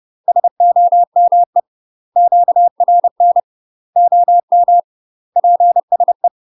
Morse code, heard as a keyed tone, about 30 wpm, 700 Hz. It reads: SOME QRN OM PSE
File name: UK MASTER QSOs TRAINING / 30WPM / UK MASTER QSOs 0009